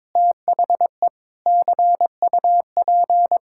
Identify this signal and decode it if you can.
Morse code, heard as a keyed tone, about 22 wpm, 700 Hz. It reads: THE CUP
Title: Sentences from Top 1600 Words 22wpm